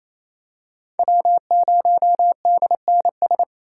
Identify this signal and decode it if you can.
Morse code, heard as a keyed tone, about 28 wpm, 700 Hz. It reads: W0DNS
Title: USA Call Signs - 8x Character Spacing - 28wpm